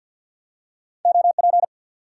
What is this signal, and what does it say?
Morse code, heard as a keyed tone, about 53 wpm, 700 Hz. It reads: OP
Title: Top 90 QSO Elements 50wpm